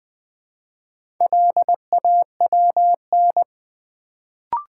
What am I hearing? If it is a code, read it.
Morse code, heard as a keyed tone, about 20 wpm, 700 Hz. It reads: LAWN
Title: Build A Word - 1 Letter At A Time - 3 to 5-Letter Words - 20wpm